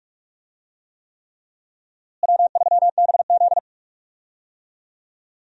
Morse code, heard as a keyed tone, about 46 wpm, 700 Hz. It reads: W3BZ
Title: USA Call Signs - 4 Characters - Repeated 3x - 45wpm